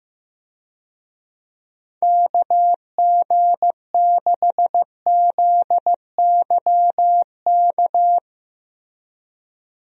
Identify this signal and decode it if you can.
Morse code, heard as a keyed tone, about 15 wpm, 700 Hz. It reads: KG6ZYK
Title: USA Call Signs - 6 Characters - 15wpm